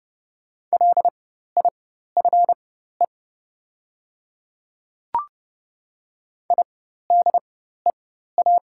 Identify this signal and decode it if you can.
Morse code, heard as a keyed tone, about 30 wpm, spaced as for 14 wpm, 700 Hz. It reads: LIFE IDEA
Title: 4x Character Spacing - Top 300 Words - 30wpm